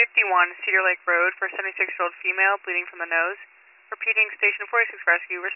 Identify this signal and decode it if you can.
Emergency Alert System (SAME) tones, heard as harsh buzzing bursts, with speech. It is heard from 1.1 to 5.2 s.